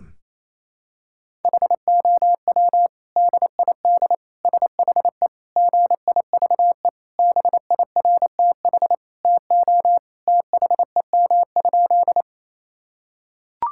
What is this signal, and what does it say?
Morse code, heard as a keyed tone, about 28 wpm, 700 Hz. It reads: HOW DID SHE GIVE BIRTH TO THEM?
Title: Sentences from Top 1000 Words - Review Part 5 - 28wpm